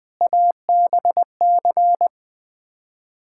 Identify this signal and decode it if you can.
Morse code, heard as a keyed tone, about 20 wpm, 700 Hz. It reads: ABC
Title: Build A Phrase - 5-Word Phrases - 20wpm